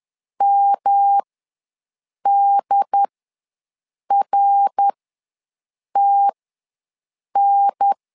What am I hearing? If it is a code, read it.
Morse code, heard as a keyed tone, about 11 wpm, spaced as for 6 wpm, 800 Hz. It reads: MDRTN
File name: LEZIONI / 3 MWRO / 3 MWR ws 10wpm 2151